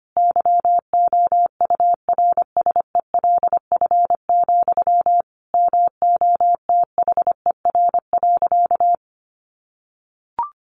Morse code, heard as a keyed tone, about 25 wpm, 700 Hz. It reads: YOURSELF, MOTHER.
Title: Sentences from Top 1300 Words 25wpm